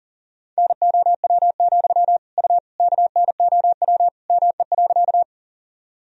Morse code, heard as a keyed tone, about 40 wpm, 700 Hz. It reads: NOW, U KNOW ME.
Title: Sentences from Top 200 Words and Abbreviations 40wpm